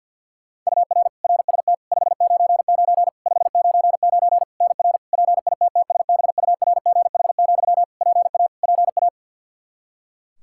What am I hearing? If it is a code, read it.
Morse code, heard as a keyed tone, about 51 wpm, 700 Hz. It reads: UR RST 599 599 NR PITTSBURGH, PA PA
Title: Practice QSO - Exchange 1 - Beginning - 50wpm